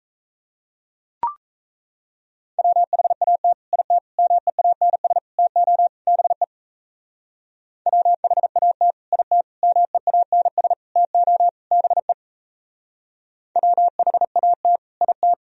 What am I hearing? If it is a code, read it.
Morse code, heard as a keyed tone, about 37 wpm, 700 Hz. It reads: WHAT IT MEANS TO BE WHAT IT MEANS TO BE WHAT IT